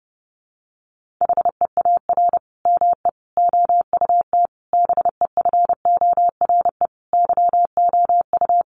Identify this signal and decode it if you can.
Morse code, heard as a keyed tone, about 28 wpm, 700 Hz. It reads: HEAR ME OUT BEFORE YOU
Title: Sentences from Top 1400 Words 30wpm